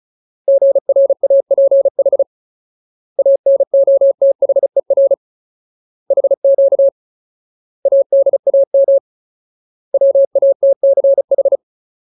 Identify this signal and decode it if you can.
Morse code, heard as a keyed tone, about 36 wpm, 550 Hz. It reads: GRAPH ANOTHER HQ ADAM WATCH
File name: Ultimate Word List / 35WPM Ultimate